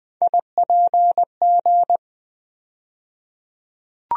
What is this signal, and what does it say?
Morse code, heard as a keyed tone, about 20 wpm, 700 Hz. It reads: IPG
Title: USA Call Sign Postfixes - 3 Letters 20wpm